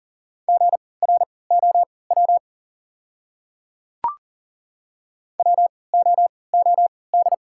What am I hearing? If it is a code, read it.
Morse code, heard as a keyed tone, about 40 wpm, spaced as for 23 wpm, 700 Hz. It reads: GROW WOOD
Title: 3x Character Spacing - Top 300 Words - 40wpm